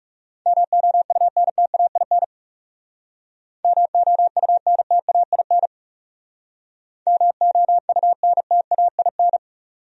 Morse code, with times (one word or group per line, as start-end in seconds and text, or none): 0.5-2.2 MOUNTAIN
3.6-5.7 MOUNTAIN
7.1-9.4 MOUNTAIN